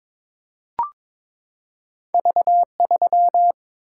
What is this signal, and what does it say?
Morse code, heard as a keyed tone, about 22 wpm, 700 Hz. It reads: V3
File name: Two Letter-Number Combinations - 22wpm